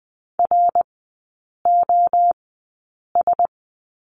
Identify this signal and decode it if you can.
Morse code, heard as a keyed tone, about 20 wpm, spaced as for 8 wpm, 700 Hz. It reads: ROS
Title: 087 Lesson 15e - Number 5 - Three Characters 20wpm